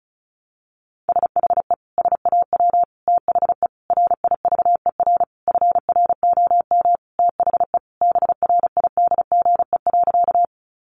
Morse code, heard as a keyed tone, about 34 wpm, 700 Hz. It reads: SHE SAW THE RIVER FROM THE BRIDGE.
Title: Sentences from Top 1200 Words 35wpm